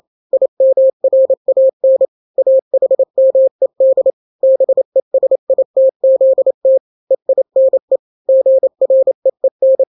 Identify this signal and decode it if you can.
Morse code, heard as a keyed tone, about 28 wpm, 530 Hz. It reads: IMRAN AHMED BESITZT EINE GREEN
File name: daily / 27wpm